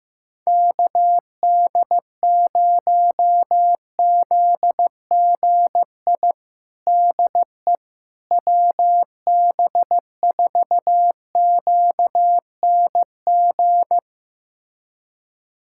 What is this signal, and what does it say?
Morse code, heard as a keyed tone, about 15 wpm, 700 Hz. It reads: KD0ZGI DE WB4QNG
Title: USA Callsign DE Callsign - 6 Characters - 15wpm